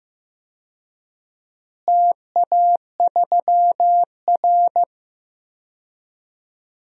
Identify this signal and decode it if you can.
Morse code, heard as a keyed tone, about 15 wpm, 700 Hz. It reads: TA3R
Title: DXCC-100 Call Signs - 4 Characters - Repeated 3x - 15wpm